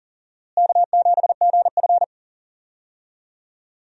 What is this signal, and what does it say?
Morse code, heard as a keyed tone, about 40 wpm, 700 Hz. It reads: K7GF
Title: USA Call Signs - 4 Characters - Repeated 3x - 40wpm